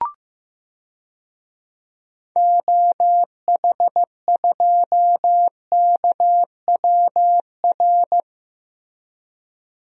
Morse code, heard as a keyed tone, about 15 wpm, 700 Hz. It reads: OH2KWR
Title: DXCC-100 Call Signs - 6 Characters - Repeated 2x - 15wpm